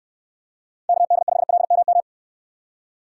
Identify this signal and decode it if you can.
Morse code, heard as a keyed tone, about 64 wpm, 700 Hz. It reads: KD5LMM